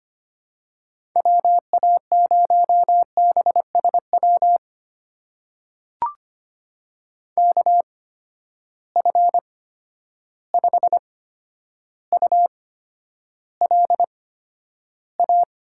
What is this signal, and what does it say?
Morse code, heard as a keyed tone, about 25 wpm, spaced as for 11 wpm, 700 Hz. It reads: WA0BSW KF5ULA
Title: USA Call Signs - 6 Characters - 8x Character Spacing - 25wpm